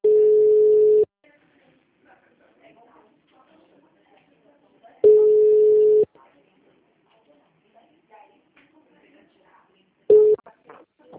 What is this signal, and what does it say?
Ringback tone (Europe, 425 Hz), with speech.
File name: out- -2007- - - .31446